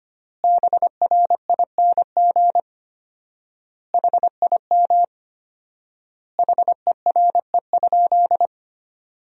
Morse code, heard as a keyed tone, about 25 wpm, 700 Hz. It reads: BRING HIM HERE?
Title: 4x Word-Spacing - Sentences from Top 500 Words - 25wpm